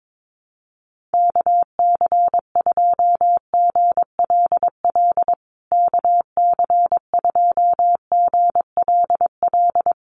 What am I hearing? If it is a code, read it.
Morse code, heard as a keyed tone, about 22 wpm, 700 Hz. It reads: KC2GLL KC2GLL
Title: USA Call Signs Repeated 3x 22wpm